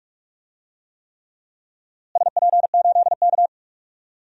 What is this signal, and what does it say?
Morse code, heard as a keyed tone, about 46 wpm, 700 Hz. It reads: SP8K
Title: DXCC-100 Call Signs - 4 Characters - Repeated 3x - 45wpm